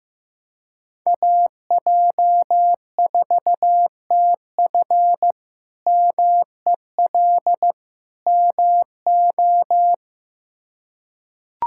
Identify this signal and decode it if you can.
Morse code, heard as a keyed tone, about 16 wpm, 700 Hz. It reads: AJ4TF MEL MO
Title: K1USN SST QSO Practice - 15wpm